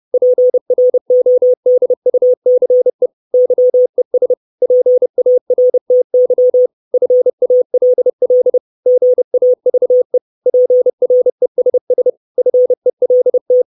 Morse code, heard as a keyed tone, about 30 wpm, 500 Hz. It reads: PRODUCE YES PARTY FALL GAVE PRESS FELT
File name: 632 Words 30WPM